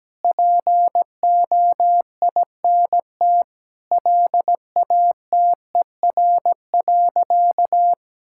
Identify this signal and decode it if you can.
Morse code, heard as a keyed tone, about 17 wpm, 700 Hz. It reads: POINT LATER.
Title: Sentences from Top 1000 Words - Review Part 6 - 17wpm